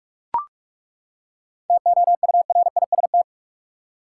Morse code, heard as a keyed tone, about 42 wpm, 700 Hz. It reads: TOURIST
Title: Seven Letter Words - 45wpm